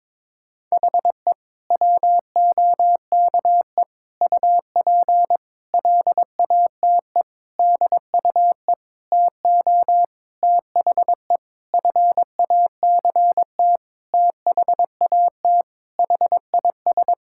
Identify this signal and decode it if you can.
Morse code, heard as a keyed tone, about 22 wpm, 700 Hz. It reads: HE WOKE UP LATE DUE TO THE FACT THAT HIS
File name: practice / 22wpm / Sentences with 4-Word Phrases - 22wpm